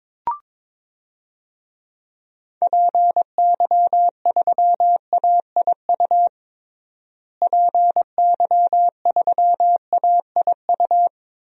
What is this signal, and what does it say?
Morse code, heard as a keyed tone, about 22 wpm, 700 Hz. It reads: PY3AIU PY3AIU
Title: DXCC-100 Call Signs - 6 Characters - Repeated 3x - 22wpm